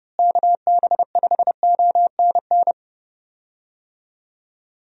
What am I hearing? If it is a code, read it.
Morse code, heard as a keyed tone, about 29 wpm, 700 Hz. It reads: KB5ONN
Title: USA Call Signs - 6 Characters - Repeated 3x - 30wpm